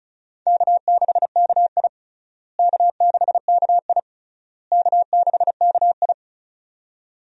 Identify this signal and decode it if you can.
Morse code, heard as a keyed tone, about 36 wpm, 700 Hz. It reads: K6KI K6KI K6KI
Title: USA Call Signs - 4 Characters - Repeated 3x - 35wpm